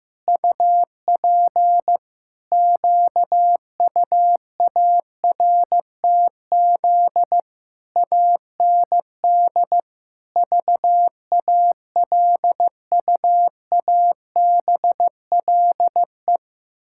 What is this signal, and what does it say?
Morse code, heard as a keyed tone, about 16 wpm, 700 Hz. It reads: UP QUARTZ AND VALUABLE
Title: Pangram Challenge v2 - 15wpm